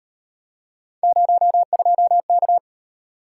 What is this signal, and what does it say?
Morse code, heard as a keyed tone, about 37 wpm, 700 Hz. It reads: 02K